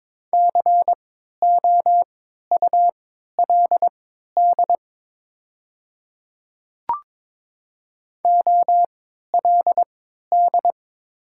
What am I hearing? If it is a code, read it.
Morse code, heard as a keyed tone, about 23 wpm, spaced as for 13 wpm, 700 Hz. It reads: COULD OLD
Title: 3x Character Spacing - Top 200 Words - 22wpm